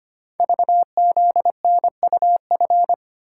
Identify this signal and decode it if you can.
Morse code, heard as a keyed tone, about 25 wpm, 700 Hz. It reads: VZNUF
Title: Code Groups - 5 Letter:Numbers - 25wpm